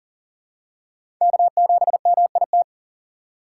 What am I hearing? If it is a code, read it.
Morse code, heard as a keyed tone, about 39 wpm, 700 Hz. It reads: K7MIT
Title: USA Call Signs - 5 Characters - 6x Character Spacing - 40wpm